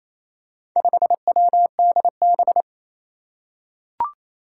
Morse code, heard as a keyed tone, about 27 wpm, 700 Hz. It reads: 5WDB
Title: Four Challenging Letter-Number Combinations 28wpm